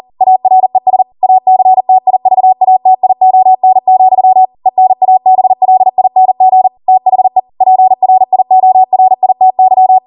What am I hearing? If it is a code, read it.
Morse code, heard as a keyed tone, about 40 wpm, 750 Hz. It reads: ARES ACTIVATION, ENABLING THE PRIORITY
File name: Morse / 40WPM